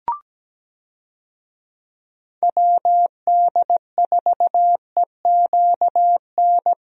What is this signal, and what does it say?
Morse code, heard as a keyed tone, about 17 wpm, 700 Hz. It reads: WD4EQN
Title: USA Call Signs - 6 Characters - Repeated 3x - 17wpm